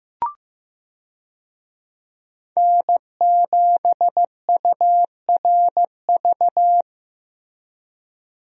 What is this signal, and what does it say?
Morse code, heard as a keyed tone, about 15 wpm, 700 Hz. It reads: N7URV
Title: USA Call Signs - 5 Characters - Repeated 3x - 15wpm